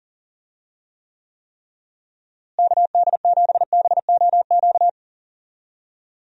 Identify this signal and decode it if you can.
Morse code, heard as a keyed tone, about 37 wpm, 700 Hz. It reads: KD7BOQ
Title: USA Call Signs - 6 Characters - Repeated 3x - 40wpm